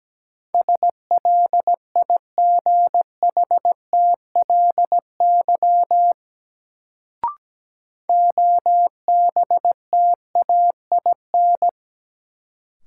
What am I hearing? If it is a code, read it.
Morse code, heard as a keyed tone, about 17 wpm, 700 Hz. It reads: SLIGHTLY OBTAIN
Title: Sentences from Top 1300 Words - New Word Drill - 17wpm